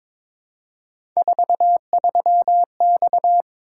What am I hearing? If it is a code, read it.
Morse code, heard as a keyed tone, about 22 wpm, 700 Hz. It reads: 43X
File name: practice / 22wpm / Three Letter-Number Combinations - 22wpm